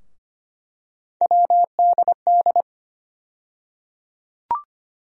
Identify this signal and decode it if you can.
Morse code, heard as a keyed tone, about 25 wpm, 700 Hz. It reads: WDD
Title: USA Call Sign Postfixes - 3 Letters 25wpm